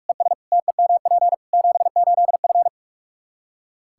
Morse code, heard as a keyed tone, about 46 wpm, 700 Hz. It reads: ES TEMP 78F
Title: K1USN SST QSO Practice 45wpm